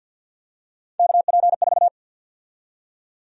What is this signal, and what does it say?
Morse code, heard as a keyed tone, about 51 wpm, 700 Hz. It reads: KP4
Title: USA Call Sign Prefixes - 50wpm